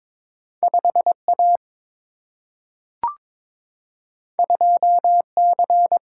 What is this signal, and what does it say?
Morse code, heard as a keyed tone, about 21 wpm, 700 Hz. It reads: 5A 2C
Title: Single-Letter Single-Number 22wpm